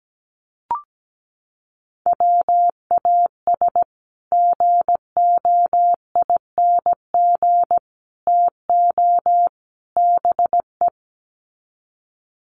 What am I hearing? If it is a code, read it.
Morse code, heard as a keyed tone, about 17 wpm, 700 Hz. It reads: WAS GOING TO BE
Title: Sets of 4 Words 17wpm